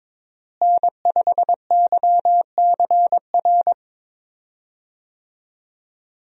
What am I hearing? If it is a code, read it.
Morse code, heard as a keyed tone, about 22 wpm, 700 Hz. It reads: N5YCR
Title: USA Call Signs - 5 Characters - Repeated 2x - 22wpm